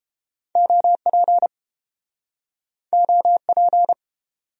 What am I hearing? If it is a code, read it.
Morse code, heard as a keyed tone, about 29 wpm, 700 Hz. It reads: OP OP